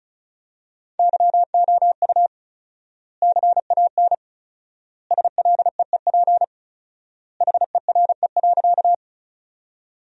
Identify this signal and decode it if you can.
Morse code, heard as a keyed tone, about 36 wpm, 700 Hz. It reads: YOU CAN SLEEP HERE.
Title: 4x Word-Spacing - Sentences from Top 500 Words - 35wpm